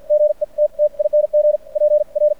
Morse code, heard as a keyed tone, about 50 wpm, spaced as for 41 wpm, 600 Hz. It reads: GETTING PR
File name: archived / CRSnet